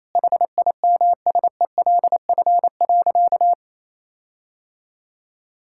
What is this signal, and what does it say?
Morse code, heard as a keyed tone, about 28 wpm, 700 Hz. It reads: HIMSELF.
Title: Sentences from Top 1600 Words 28wpm